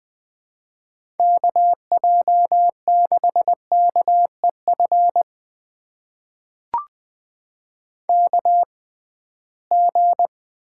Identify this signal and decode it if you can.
Morse code, heard as a keyed tone, about 20 wpm, spaced as for 15 wpm, 700 Hz. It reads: KJ6KEF KG
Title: USA Call Signs - 6 Characters - 6x Character Spacing - 20wpm